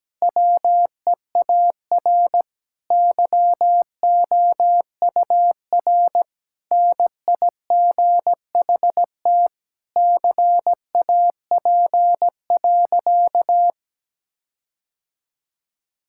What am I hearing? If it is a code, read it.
Morse code, heard as a keyed tone, about 17 wpm, 700 Hz. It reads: WEAR YOUR NIGHT CAP.
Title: Sentences from Top 1600 Words 17wpm